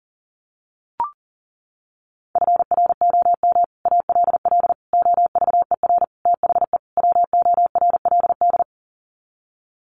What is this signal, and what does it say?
Morse code, heard as a keyed tone, about 40 wpm, 700 Hz. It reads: FROM ALL OVER THE WORLD
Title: Sets of 5 Words 40wpm